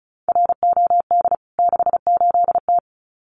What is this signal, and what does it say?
Morse code, heard as a keyed tone, about 34 wpm, 700 Hz. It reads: ROD 68T
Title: CWT Exchange 35wpm